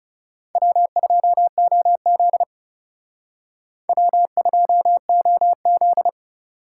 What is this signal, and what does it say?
Morse code, heard as a keyed tone, about 36 wpm, 700 Hz. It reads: W2OZ W2OZ
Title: Speed Racing - USA 1x2 Call Signs 30wpm